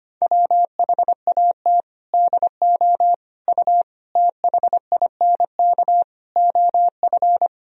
Morse code, heard as a keyed tone, about 25 wpm, 700 Hz. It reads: WHAT DO U THINK OF
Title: Sentences from Top 300 Words and Abbreviations 25wpm